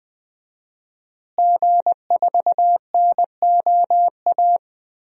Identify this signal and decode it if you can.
Morse code, heard as a keyed tone, about 20 wpm, 700 Hz. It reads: G4NOA
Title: DXCC-100 Call Signs - 5 Characters - Repeated 3x - 20wpm